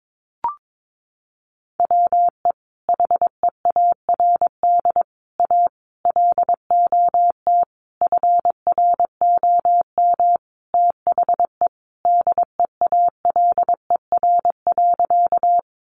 Morse code, heard as a keyed tone, about 22 wpm, 700 Hz. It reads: WE HEARD A LOT FROM THE DEALER.
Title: Sentences from Top 1100 Words 22wpm